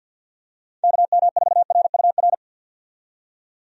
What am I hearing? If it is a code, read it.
Morse code, heard as a keyed tone, about 53 wpm, 700 Hz. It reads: KM4RUR
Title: USA Call Signs - 6 Characters - 50wpm